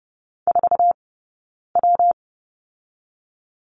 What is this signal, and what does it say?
Morse code, heard as a keyed tone, about 30 wpm, 700 Hz. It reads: 4 W
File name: Lesson 19d - Letter w - Two Characters 30wpm